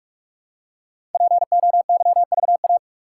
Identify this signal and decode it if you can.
Morse code, heard as a keyed tone, about 44 wpm, 700 Hz. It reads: POYVA
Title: Code Groups - 5 Letters - 45wpm